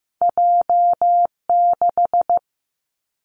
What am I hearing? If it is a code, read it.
Morse code, heard as a keyed tone, about 15 wpm, 700 Hz. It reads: J 6
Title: Two Challenging Letter-Number Combinations 15wpm